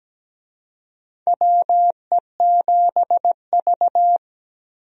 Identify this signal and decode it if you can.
Morse code, heard as a keyed tone, about 17 wpm, 700 Hz. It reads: WE7V
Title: USA Call Signs - 4 Characters - Repeated 2x - 17wpm